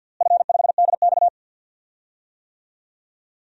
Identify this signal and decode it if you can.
Morse code, heard as a keyed tone, about 51 wpm, 700 Hz. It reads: F5DX
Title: DXCC-100 Call Signs - 4 Characters - Repeated 3x - 50wpm